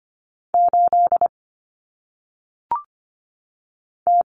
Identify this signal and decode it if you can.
Morse code, heard as a keyed tone, about 25 wpm, 700 Hz. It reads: 8 T
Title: Common Cut Numbers - 1-Digit 25wpm